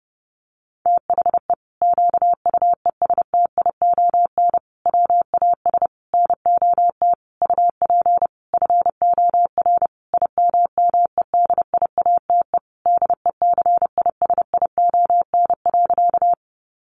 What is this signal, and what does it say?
Morse code, heard as a keyed tone, about 30 wpm, 700 Hz. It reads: THE QUESTION WAS NOT UP FOR IMMEDIATE DECISION.
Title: Sentences from Top 1200 Words 30wpm